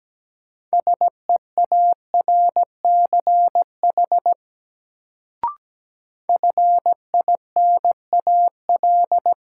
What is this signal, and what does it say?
Morse code, heard as a keyed tone, about 17 wpm, 700 Hz. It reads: SEARCH FINAL
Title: Sentences from Top 800 Words - New Word Drill - 17wpm